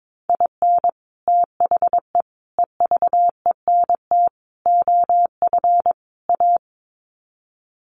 Morse code, heard as a keyed tone, about 22 wpm, 700 Hz. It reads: IN THE EVENT OF A
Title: Sets of 5 Words Encore 22wpm